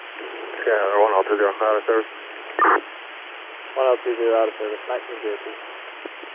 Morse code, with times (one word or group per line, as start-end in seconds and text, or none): none